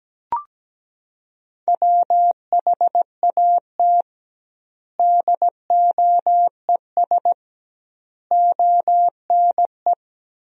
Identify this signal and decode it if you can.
Morse code, heard as a keyed tone, about 17 wpm, 700 Hz. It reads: WHAT DOES ONE
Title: 2x Word-Spacing - Sentences from Top 200 Words - 17wpm